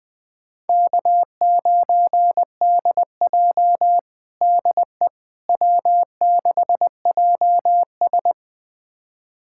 Morse code, heard as a keyed tone, about 20 wpm, 700 Hz. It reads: K9DJ DE W6JS
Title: USA Callsign DE Callsign - 4 Characters - 20wpm